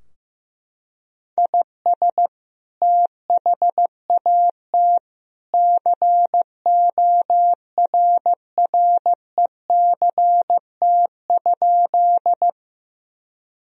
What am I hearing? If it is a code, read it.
Morse code, heard as a keyed tone, about 16 wpm, 700 Hz. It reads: IS THAT CORRECT?
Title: Sentences from Top 1000 Words - Review Part 2 - 15wpm